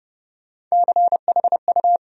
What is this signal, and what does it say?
Morse code, heard as a keyed tone, about 32 wpm, 700 Hz. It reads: CHU
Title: USA Call Sign Postfixes - 3 Letters 30wpm